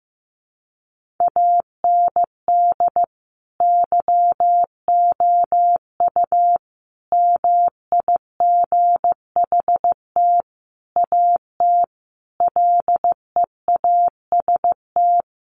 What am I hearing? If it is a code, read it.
Morse code, heard as a keyed tone, about 15 wpm, 700 Hz. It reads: AND YOU MIGHT AT LEAST
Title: Sentences from Top 1500 Words 15wpm